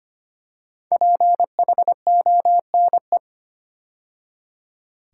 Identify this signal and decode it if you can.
Morse code, heard as a keyed tone, about 25 wpm, 700 Hz. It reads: PHONE
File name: 5 Letter Words - First 3 Letters Given - 25wpm